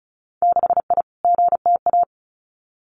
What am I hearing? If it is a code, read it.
Morse code, heard as a keyed tone, about 34 wpm, 700 Hz. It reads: 6I GTA
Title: Winter Field Day Exchanges Encore 35wpm